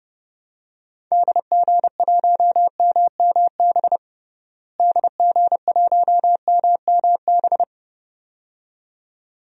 Morse code, heard as a keyed tone, about 30 wpm, 700 Hz. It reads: DG1MMB DG1MMB
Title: DXCC-100 Call Signs - 6 Characters - Repeated 2x - 30wpm